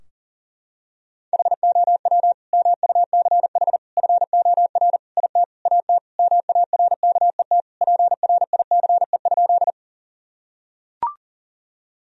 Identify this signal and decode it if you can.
Morse code, heard as a keyed tone, about 37 wpm, 700 Hz. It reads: HOW MUCH FOR IT AT MARKET PRICE?
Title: Sentences from Top 1000 Words - Review Part 3 - 40wpm